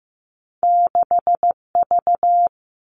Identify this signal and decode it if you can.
Morse code, heard as a keyed tone, about 15 wpm, 700 Hz. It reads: 6V
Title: Two Challenging Letter-Number Combinations 15wpm